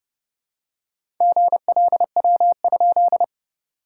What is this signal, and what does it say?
Morse code, heard as a keyed tone, about 30 wpm, 700 Hz. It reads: GLW?
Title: SOTA QSO Practice 30wpm